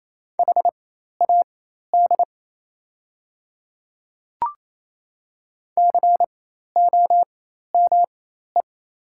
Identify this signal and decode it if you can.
Morse code, heard as a keyed tone, about 28 wpm, spaced as for 13 wpm, 700 Hz. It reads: HAD COME
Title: 4x Character Spacing - Top 100 Words - 28wpm